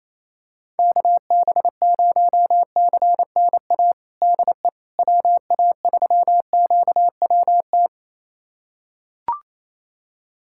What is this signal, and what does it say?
Morse code, heard as a keyed tone, about 28 wpm, 700 Hz. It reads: KB0CNA DE WA3QWT
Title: USA Callsign DE Callsign - 6 Characters - 28wpm